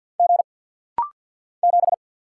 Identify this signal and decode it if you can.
Morse code, heard as a keyed tone, about 48 wpm, 700 Hz. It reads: G 7
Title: Single Letter-Number - Rapid-Fire 50wpm